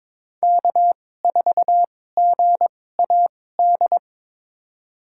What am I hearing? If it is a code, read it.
Morse code, heard as a keyed tone, about 22 wpm, spaced as for 16 wpm, 700 Hz. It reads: K4GAD
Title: USA Call Signs - 2x Character Spacing - 22wpm